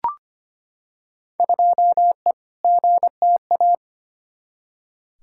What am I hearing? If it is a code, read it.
Morse code, heard as a keyed tone, about 25 wpm, 700 Hz. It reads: 2E GTA
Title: Field Day Exchanges Encore - 25wpm